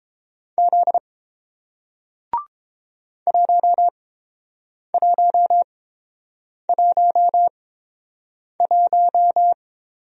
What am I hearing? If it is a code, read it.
Morse code, heard as a keyed tone, about 29 wpm, spaced as for 7 wpm, 700 Hz. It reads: Z 1111